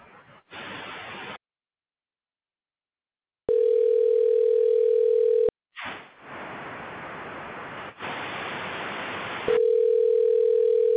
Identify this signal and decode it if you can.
Ringback tone (North America), 440 + 480 Hz.